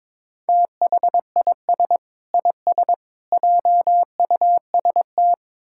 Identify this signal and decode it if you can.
Morse code, heard as a keyed tone, about 22 wpm, 700 Hz. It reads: THIS IS JUST
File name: Sentences with 7-Word Phrases - 22wpm